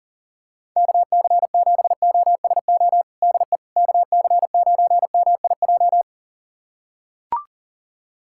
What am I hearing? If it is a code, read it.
Morse code, heard as a keyed tone, about 37 wpm, 700 Hz. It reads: KC7OSO DE KC9MIJ
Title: USA Callsign DE Callsign - 6 Characters - 40wpm